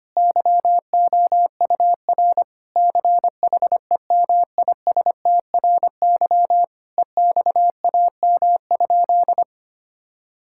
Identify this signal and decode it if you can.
Morse code, heard as a keyed tone, about 25 wpm, 700 Hz. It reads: YOUR CHEMISTRY EXAM?
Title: Sentences from Top 1000 Words - Review Part 6 - 25wpm